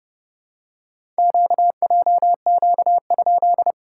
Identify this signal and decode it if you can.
Morse code, heard as a keyed tone, about 30 wpm, 700 Hz. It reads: QJQ?
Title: SOTA QSO Practice 30wpm